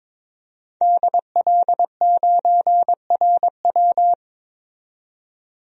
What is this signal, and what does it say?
Morse code, heard as a keyed tone, about 21 wpm, 700 Hz. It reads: DL9RW
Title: DXCC-100 Call Signs - 5 Characters - 22wpm